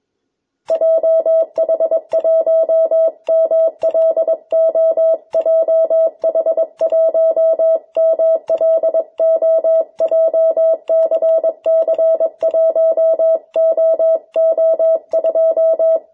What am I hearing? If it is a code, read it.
Morse code, heard as a keyed tone, about 22 wpm, 650 Hz. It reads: JH1MLO JH1MLO JCC1OO2
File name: jh8cbh / 2021cw